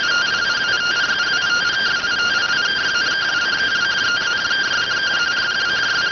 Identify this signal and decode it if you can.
Two-tone teleprinter signal, about 170 Hz shift.